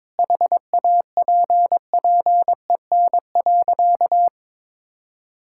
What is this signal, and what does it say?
Morse code, heard as a keyed tone, about 22 wpm, 700 Hz. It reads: HAPPEN.
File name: practice / 22wpm / Sentences with 2-Word Phrases - 22wpm